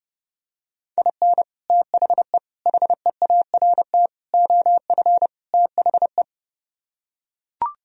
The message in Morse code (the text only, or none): IN THE HEART OF THE